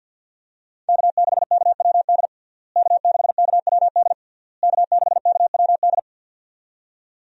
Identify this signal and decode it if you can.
Morse code, heard as a keyed tone, about 51 wpm, 700 Hz. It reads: K6KWD K6KWD K6KWD
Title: USA Call Signs - 5 Characters - Repeated 3x - 50wpm